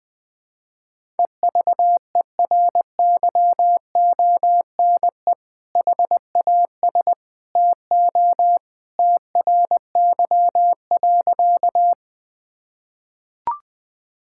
Morse code, heard as a keyed tone, about 20 wpm, 700 Hz. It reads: EVERYONE HAS TO TRY.
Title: Sentences with 2-Word Phrases - 20wpm